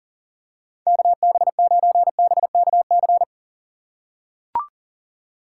Morse code, heard as a keyed tone, about 40 wpm, 700 Hz. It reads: KB9BKC
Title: USA Call Signs - 6 Characters - 4x Character Spacing - 40wpm